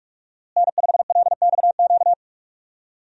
Morse code, heard as a keyed tone, about 44 wpm, 700 Hz. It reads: N5LXQ
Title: USA Call Signs - 5 Characters - Repeated 2x - 45wpm